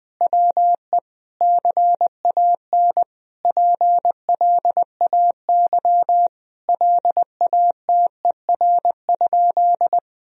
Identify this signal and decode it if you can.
Morse code, heard as a keyed tone, about 20 wpm, 700 Hz. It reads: WE CAN PLAY LATER?
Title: Sentences from Top 1000 Words - Review Part 4 - 20wpm